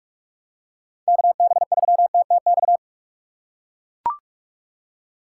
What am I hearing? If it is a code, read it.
Morse code, heard as a keyed tone, about 43 wpm, 700 Hz. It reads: KB3TTX
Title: USA Call Signs - 6 Characters - 4x Character Spacing - 45wpm